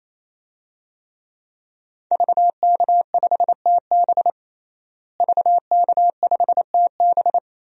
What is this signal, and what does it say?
Morse code, heard as a keyed tone, about 28 wpm, 700 Hz. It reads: VK5TB VK5TB
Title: DXCC-100 Call Signs - 5 Characters - Repeated 3x - 28wpm